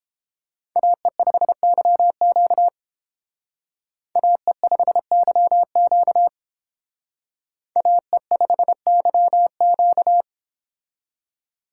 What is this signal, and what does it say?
Morse code, heard as a keyed tone, about 30 wpm, 700 Hz. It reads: AE5YQ AE5YQ AE5YQ